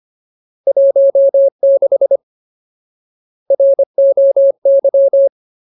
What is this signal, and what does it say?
Morse code, heard as a keyed tone, about 25 wpm, 550 Hz. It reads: 16 ROY
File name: Ultimate Word List / 25WPM Ultimate